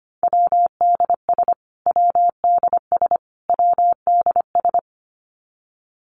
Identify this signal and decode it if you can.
Morse code, heard as a keyed tone, about 25 wpm, 700 Hz. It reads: WDS WDS WDS
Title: Abbreviations Repeated 3x - 25wpm